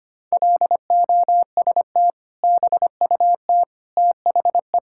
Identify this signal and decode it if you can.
Morse code, heard as a keyed tone, about 25 wpm, 700 Hz. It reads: LOST BUT THE
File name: practice / 25wpm / Sentences with 2-Word Phrases - 25wpm